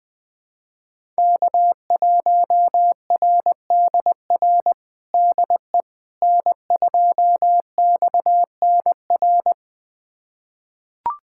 Morse code, heard as a keyed tone, about 20 wpm, 700 Hz. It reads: K1RDR DE N2XNR
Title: USA Callsign DE Callsign - 5 Characters - 20wpm